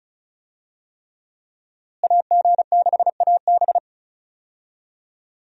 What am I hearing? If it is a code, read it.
Morse code, heard as a keyed tone, about 36 wpm, 700 Hz. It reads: AG6AB
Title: USA Call Signs - 5 Characters - Repeated 3x - 35wpm